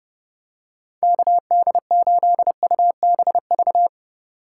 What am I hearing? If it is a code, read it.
Morse code, heard as a keyed tone, about 30 wpm, 700 Hz. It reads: KD8UBV
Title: USA Call Signs - 6 Characters - 2x Character Spacing - 30wpm